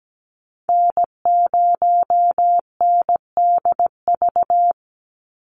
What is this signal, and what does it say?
Morse code, heard as a keyed tone, about 17 wpm, 700 Hz. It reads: N0NDV
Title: USA Call Signs 17wpm